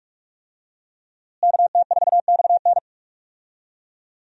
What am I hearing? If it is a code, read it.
Morse code, heard as a keyed tone, about 41 wpm, 700 Hz. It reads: KT4XN
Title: USA Call Signs - 5 Characters - Repeated 3x - 45wpm